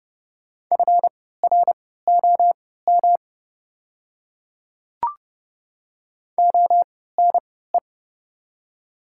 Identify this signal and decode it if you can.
Morse code, heard as a keyed tone, about 30 wpm, spaced as for 17 wpm, 700 Hz. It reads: FROM ONE
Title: 3x Character Spacing - Top 100 Words - 30wpm